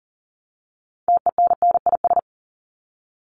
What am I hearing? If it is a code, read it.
Morse code, heard as a keyed tone, about 40 wpm, 700 Hz. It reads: TENNIS
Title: Top 1501-1750 Words 40wpm